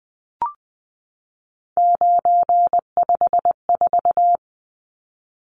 Morse code, heard as a keyed tone, about 20 wpm, 700 Hz. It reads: 954
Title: Numbers - 3-Digit 20wpm